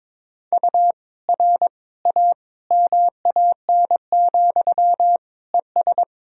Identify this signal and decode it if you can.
Morse code, heard as a keyed tone, about 22 wpm, 700 Hz. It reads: U R A MAN, ES
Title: Sentences from Top 200 Words and Abbreviations 22wpm